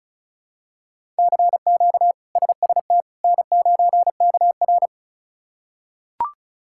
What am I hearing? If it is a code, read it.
Morse code, heard as a keyed tone, about 34 wpm, 700 Hz. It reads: CQ SST N9KR
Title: K1USN SST QSO Practice - 35wpm